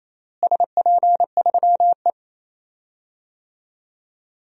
Morse code, heard as a keyed tone, about 28 wpm, 700 Hz. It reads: SP3E
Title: DXCC-100 Call Signs - 4 Characters - Repeated 3x - 28wpm